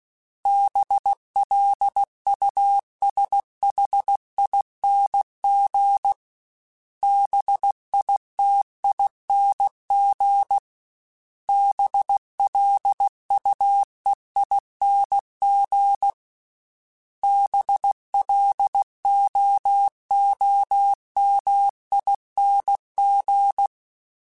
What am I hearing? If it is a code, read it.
Morse code, heard as a keyed tone, about 16 wpm, 800 Hz. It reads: BLUSHING BITING BLUEING BLOOMING
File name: Special INGs / Special INGs-mp3s / Sngl ing @15